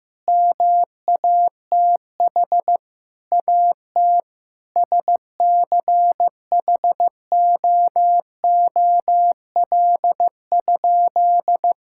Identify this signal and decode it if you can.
Morse code, heard as a keyed tone, about 15 wpm, 700 Hz. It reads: MATH AT SCHOOL?
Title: Sentences from Top 1000 Words - Review Part 6 - 15wpm